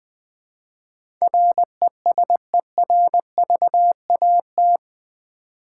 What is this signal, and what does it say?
Morse code, heard as a keyed tone, about 20 wpm, 700 Hz. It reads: RESERVAT
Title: Build A Word - 4 Letters At A Time - 11-Letter Words - 20wpm